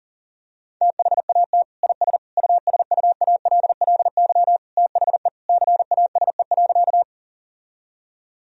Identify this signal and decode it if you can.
Morse code, heard as a keyed tone, about 40 wpm, 700 Hz. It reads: THAT IS USUALLY THE CASE.
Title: Sentences from Top 1000 Words - Review Part 3 - 40wpm